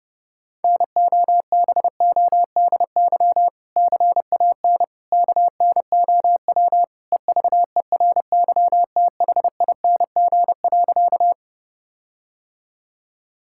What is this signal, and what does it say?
Morse code, heard as a keyed tone, about 32 wpm, 700 Hz. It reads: NOBODY CAN KNOW EVERYTHING.
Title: Sentences from Top 1600 Words 30wpm